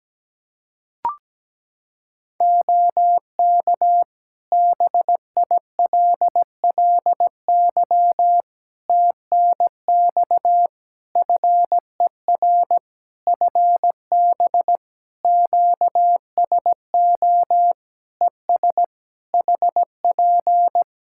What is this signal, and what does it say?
Morse code, heard as a keyed tone, about 17 wpm, 700 Hz. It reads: OK BILLY TNX FER FB QSO ES HP
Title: Practice QSO - Exchange 4 - Ending - 17wpm